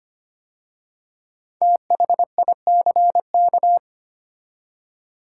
Morse code, heard as a keyed tone, about 25 wpm, 700 Hz. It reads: THICK
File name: 5 Letter Words - First 3 Letters Given - 25wpm